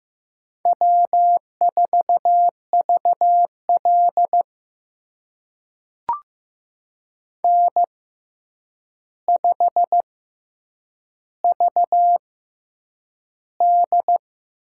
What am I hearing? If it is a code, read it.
Morse code, heard as a keyed tone, about 15 wpm, spaced as for 8 wpm, 700 Hz. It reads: W4VL N5VD
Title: USA Call Signs - 4 Characters - 6x Character Spacing - 15wpm